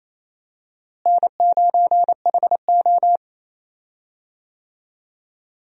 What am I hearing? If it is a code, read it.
Morse code, heard as a keyed tone, about 28 wpm, 700 Hz. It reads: N9HO
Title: USA Call Signs - 4 Characters - Repeated 2x - 28wpm